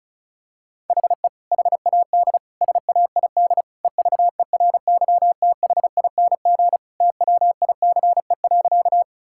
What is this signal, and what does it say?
Morse code, heard as a keyed tone, about 36 wpm, 700 Hz. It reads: HE HAD SAID EVERYTHING TWICE.
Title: Sentences from Top 1600 Words 35wpm